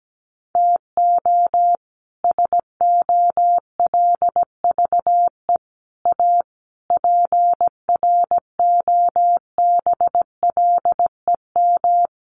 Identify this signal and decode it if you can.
Morse code, heard as a keyed tone, about 17 wpm, 700 Hz. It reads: TO SOLVE A PROBLEM
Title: Sentences from Top 1400 Words 17wpm